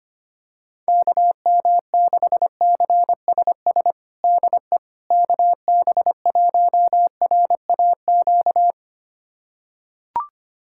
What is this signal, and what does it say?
Morse code, heard as a keyed tone, about 25 wpm, 700 Hz. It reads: KM6CSS DE KB1RAQ
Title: USA Callsign DE Callsign 25wpm